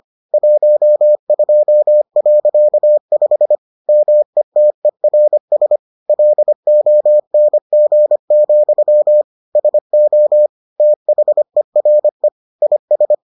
Morse code, heard as a keyed tone, about 25 wpm, 600 Hz. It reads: 12.5 METERS LONG, SO THERE IS